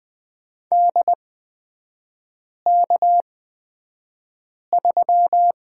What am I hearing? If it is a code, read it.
Morse code, heard as a keyed tone, about 20 wpm, spaced as for 5 wpm, 700 Hz. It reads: DK3
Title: 243 Lesson 39e - Letter z - Three Characters 20wpm